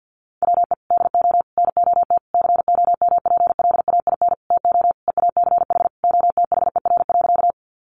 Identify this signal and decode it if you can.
Morse code, heard as a keyed tone, about 50 wpm, 700 Hz. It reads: WE DO NOT COMPLAIN TO EACH OTHER.
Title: Sentences from Top 1300 Words 50wpm